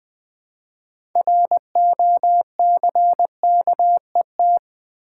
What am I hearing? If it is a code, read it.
Morse code, heard as a keyed tone, about 20 wpm, 700 Hz. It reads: ROCKET
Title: Build A Word - 2 Letters At A Time - 6-Letter Words - 20wpm